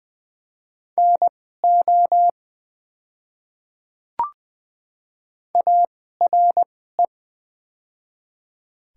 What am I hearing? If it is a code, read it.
Morse code, heard as a keyed tone, about 20 wpm, spaced as for 15 wpm, 700 Hz. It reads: NO ARE
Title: 2x Character Spacing - Top 100 Words - 20wpm